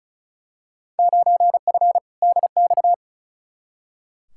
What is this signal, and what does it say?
Morse code, heard as a keyed tone, about 34 wpm, 700 Hz. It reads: 9F DX
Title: Field Day Exchanges Encore - 35wpm